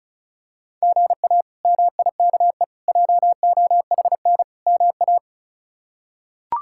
Morse code, heard as a keyed tone, about 36 wpm, 700 Hz. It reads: GA MIKE JOHN MA
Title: K1USN SST QSO Practice - 35wpm